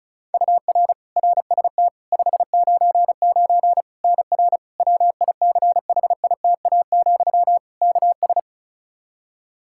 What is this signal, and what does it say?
Morse code, heard as a keyed tone, about 36 wpm, 700 Hz. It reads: UR RST 599 NR WICHITA, KS
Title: K1USN SST QSO Practice 35wpm